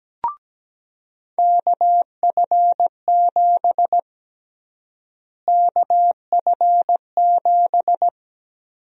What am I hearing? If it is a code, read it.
Morse code, heard as a keyed tone, about 17 wpm, 700 Hz. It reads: KF7 KF7
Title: USA Call Sign Prefixes - Repeated 2x - 17wpm